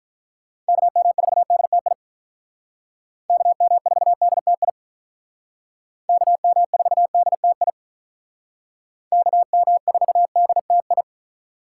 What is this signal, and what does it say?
Morse code, heard as a keyed tone, about 45 wpm, 700 Hz. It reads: KM4DTI KM4DTI KM4DTI KM4DTI